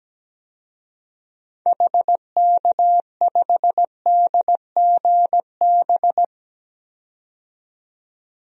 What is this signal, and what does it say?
Morse code, heard as a keyed tone, about 17 wpm, 700 Hz. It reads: HK5DGB
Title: DXCC-100 Call Signs - 6 Characters - 17wpm